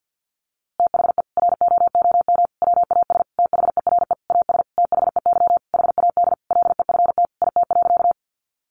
Morse code, heard as a keyed tone, about 50 wpm, 700 Hz. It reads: THE ROOM WAS THERE AS THEY HAD LEFT IT.
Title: A Sound Of Thunder 50wpm